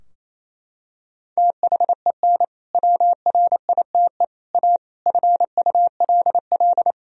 Morse code, heard as a keyed tone, about 28 wpm, 700 Hz. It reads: THEN WRITE A FULL
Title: Sentences from Top 1000 Words - Review Part 3 - 28wpm